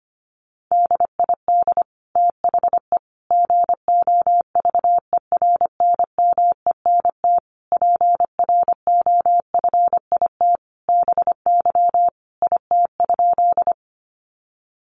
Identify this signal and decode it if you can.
Morse code, heard as a keyed tone, about 25 wpm, 700 Hz. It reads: DID THE GOVERNMENT PROFIT BY IT?
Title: Sentences from Top 800 Words 25wpm